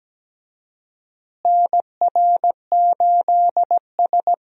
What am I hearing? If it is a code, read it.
Morse code, heard as a keyed tone, about 17 wpm, 700 Hz. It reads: NR8S
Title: USA Call Signs - 4 Characters - Repeated 2x - 17wpm